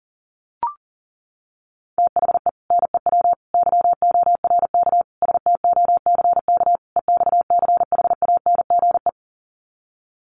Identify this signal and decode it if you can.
Morse code, heard as a keyed tone, about 37 wpm, 700 Hz. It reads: THE NEW YORK STOCK EXCHANGE
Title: Sets of 5 Words 40wpm